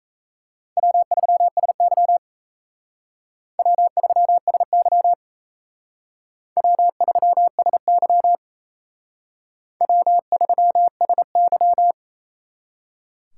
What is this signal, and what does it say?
Morse code, heard as a keyed tone, about 36 wpm, 700 Hz. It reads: W3SY W3SY W3SY W3SY